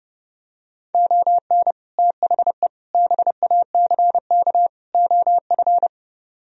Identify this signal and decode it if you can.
Morse code, heard as a keyed tone, about 32 wpm, 700 Hz. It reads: ON THE BACK OF